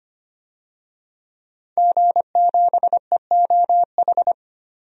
Morse code, heard as a keyed tone, about 25 wpm, 700 Hz. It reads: G7EOH
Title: DXCC-100 Call Signs - 5 Characters - 25wpm